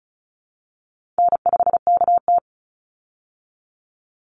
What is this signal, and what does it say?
Morse code, heard as a keyed tone, about 34 wpm, 700 Hz. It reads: N5KT
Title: Easy Call Signs Repeated 2x - 35wpm